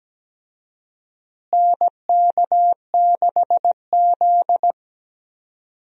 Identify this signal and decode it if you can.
Morse code, heard as a keyed tone, about 17 wpm, 700 Hz. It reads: NK6Z
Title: USA Call Signs - 4 Characters - Repeated 3x - 17wpm